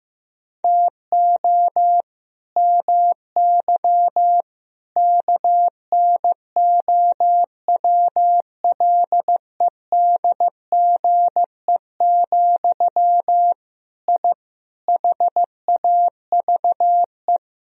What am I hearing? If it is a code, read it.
Morse code, heard as a keyed tone, about 15 wpm, 700 Hz. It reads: TO MY KNOWLEDGE, I HAVE
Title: Sentences from Top 1000 Words - Review Part 3 - 15wpm